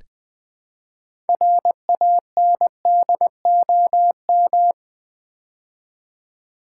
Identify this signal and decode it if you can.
Morse code, heard as a keyed tone, about 20 wpm, 700 Hz. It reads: RANDOM
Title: Build A Word - 3 Letters At A Time - 6-Letter Words - 20wpm